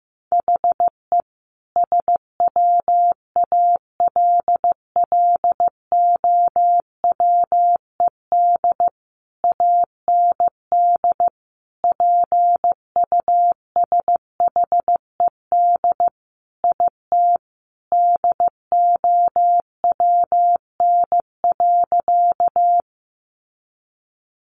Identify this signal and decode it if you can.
Morse code, heard as a keyed tone, about 15 wpm, 700 Hz. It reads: HE SWALLOWED AND PUSHED IT DOWN.
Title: A Sound Of Thunder 15wpm